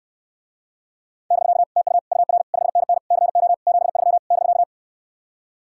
Morse code, heard as a keyed tone, about 67 wpm, 700 Hz. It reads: <BK> TU UR 5NN CO 73 <BK>